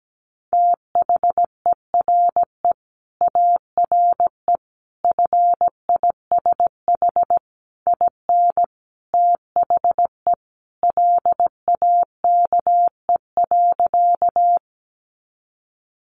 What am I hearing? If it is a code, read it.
Morse code, heard as a keyed tone, about 17 wpm, 700 Hz. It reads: THERE ARE FISH IN THE LAKE.
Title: Sentences from Top 1100 Words 17wpm